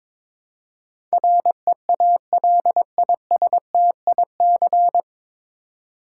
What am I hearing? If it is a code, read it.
Morse code, heard as a keyed tone, about 23 wpm, 700 Hz. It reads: REALISTIC
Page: Morse Code Ninja practice 22wpm